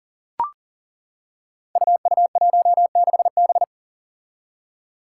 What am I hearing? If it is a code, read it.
Morse code, heard as a keyed tone, about 40 wpm, 700 Hz. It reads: UU16B
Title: Five Challenging Letter-Number Combinations 40wpm